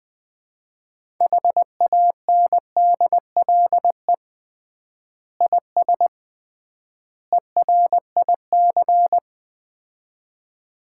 Morse code, heard as a keyed tone, about 21 wpm, 700 Hz. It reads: HANDLE IS ERIC
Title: Common QSO Phrases - 3x Word-Spacing - 20wpm